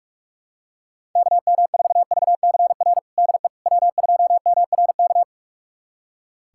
Morse code, heard as a keyed tone, about 43 wpm, 700 Hz. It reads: KM4VCR DE W2MRK
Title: USA Callsign DE Callsign 45wpm